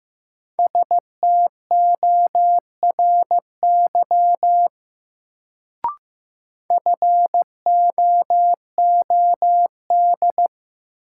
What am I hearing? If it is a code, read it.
Morse code, heard as a keyed tone, about 15 wpm, 700 Hz. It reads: STORY FOOD
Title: Sentences from Top 300 Words - New Word Drill - 15wpm